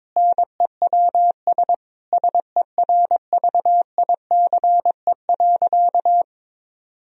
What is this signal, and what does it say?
Morse code, heard as a keyed tone, about 22 wpm, 700 Hz. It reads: NEWS SERVICE.
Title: Sentences from Top 1000 Words - Review Part 4 - 22wpm